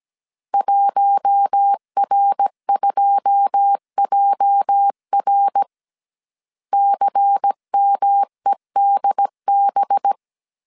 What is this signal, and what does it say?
Morse code, heard as a keyed tone, about 17 wpm, 800 Hz. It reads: 1R2JR CMEDB
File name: CW / DET / 1 9 1500C S C 16wpm 2909